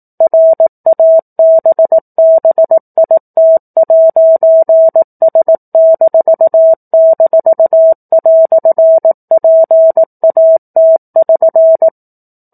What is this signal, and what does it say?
Morse code, heard as a keyed tone, about 18 wpm, 650 Hz. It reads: RABBIT'S--"PAT<SN>
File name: 04-Chapter-04 18 WPM